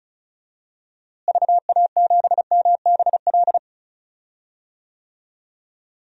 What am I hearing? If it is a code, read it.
Morse code, heard as a keyed tone, about 36 wpm, 700 Hz. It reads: VA7MBL
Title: DXCC-100 Call Signs - 6 Characters - Repeated 2x - 35wpm